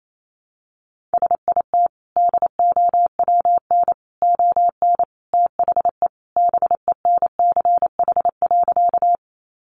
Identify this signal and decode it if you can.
Morse code, heard as a keyed tone, about 28 wpm, 700 Hz. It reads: SIT DOWN ON THE BENCH.
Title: Sentences from Top 1400 Words 28wpm